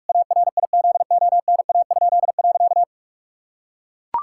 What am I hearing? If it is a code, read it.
Morse code, heard as a keyed tone, about 42 wpm, 700 Hz. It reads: ARIZONA?.
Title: Pangram Challenge v2 - 45wpm